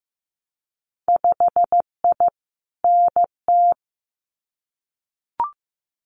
Morse code, heard as a keyed tone, about 15 wpm, 700 Hz. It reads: 5I NT
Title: Winter Field Day Exchanges Encore 15wpm